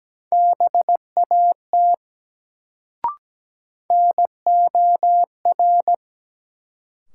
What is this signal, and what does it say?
Morse code, heard as a keyed tone, about 17 wpm, 700 Hz. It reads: BAT NOR
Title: Sentences from Top 1500 Words - New Word Drill Alt - 17wpm